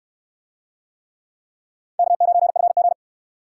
Morse code, heard as a keyed tone, about 61 wpm, 700 Hz. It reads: K9FG